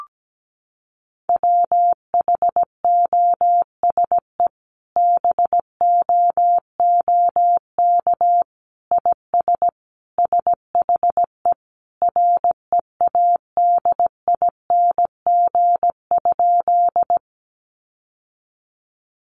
Morse code, heard as a keyed tone, about 17 wpm, 700 Hz. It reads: WHOSE BOOK IS SHE READING?
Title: Sentences from Top 1200 Words 17wpm